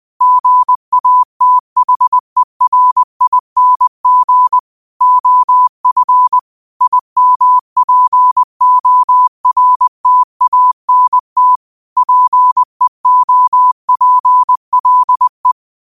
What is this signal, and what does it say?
Morse code, heard as a keyed tone, about 20 wpm, 1000 Hz. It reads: GATHERING OF IMPORTANT PEOPLE
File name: qotdfiles / 20wpm